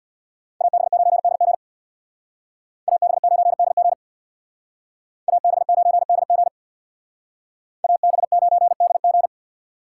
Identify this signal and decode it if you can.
Morse code, heard as a keyed tone, about 63 wpm, 700 Hz. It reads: AB9DG AB9DG AB9DG AB9DG